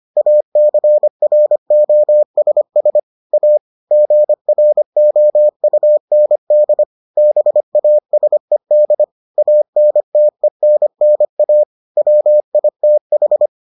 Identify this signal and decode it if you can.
Morse code, heard as a keyed tone, about 25 wpm, 600 Hz. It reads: ACROSS A GROUND BASED ANTENNA WITH